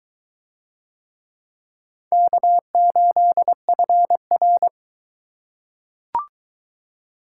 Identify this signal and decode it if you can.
Morse code, heard as a keyed tone, about 22 wpm, 700 Hz. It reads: K8FR